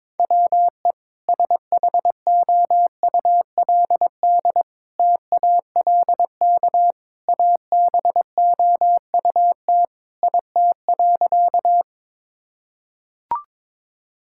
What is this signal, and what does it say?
Morse code, heard as a keyed tone, about 22 wpm, 700 Hz. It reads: WE SHOULD TALK ABOUT IT.
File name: practice / 22wpm / Sentences with 2-Word Phrases - 22wpm